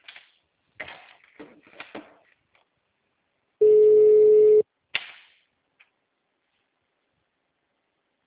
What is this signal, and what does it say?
Ringback tone (Europe, 425 Hz).
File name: out- -2003- - - .4